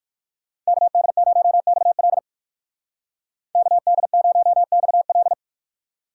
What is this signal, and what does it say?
Morse code, heard as a keyed tone, about 54 wpm, 700 Hz. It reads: KD0XL KD0XL
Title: Speed Racing - USA 2x2 Call Signs 45wpm